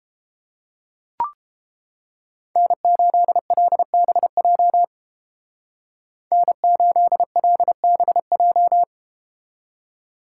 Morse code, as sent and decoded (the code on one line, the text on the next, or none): -. ---.. .-.. -... .--- / -. ---.. .-.. -... .---
N8LBJ N8LBJ